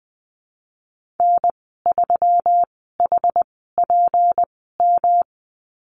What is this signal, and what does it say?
Morse code, heard as a keyed tone, about 20 wpm, spaced as for 14 wpm, 700 Hz. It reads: N3HPM
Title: 161 Lesson 26g - Letter g - Call Signs 20wpm